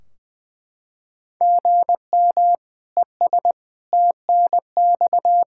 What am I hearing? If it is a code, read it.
Morse code, heard as a keyed tone, about 20 wpm, 700 Hz. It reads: GM ES TNX
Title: Practice QSO - Exchange 1 - Beginning - 20wpm